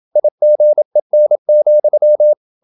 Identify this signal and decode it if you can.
Morse code, heard as a keyed tone, about 27 wpm, 600 Hz. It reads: IGEN,
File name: Ferie QRQ 27wpm